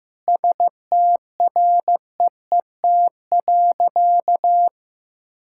Morse code, heard as a keyed tone, about 15 wpm, 700 Hz. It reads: STREET.
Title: Sentences from Top 1000 Words - Review Part 5 - 15wpm